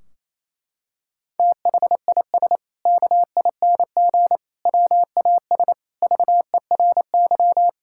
Morse code, heard as a keyed tone, about 28 wpm, 700 Hz. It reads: THIS KING WAS VERY
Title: Sentences from Top 1600 Words 28wpm